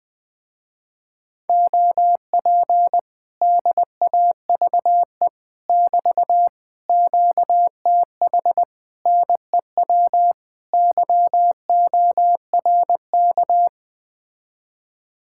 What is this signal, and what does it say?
Morse code, heard as a keyed tone, about 21 wpm, 700 Hz. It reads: OP DAVE <BT> QTH NEW YORK